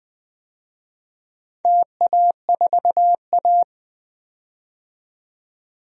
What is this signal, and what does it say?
Morse code, heard as a keyed tone, about 21 wpm, 700 Hz. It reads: TA4A
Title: DXCC-100 Call Signs - 4 Characters - 20wpm